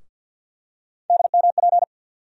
Morse code, heard as a keyed tone, about 50 wpm, 700 Hz. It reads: DMP
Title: USA Call Sign Postfixes - 3 Letters 50wpm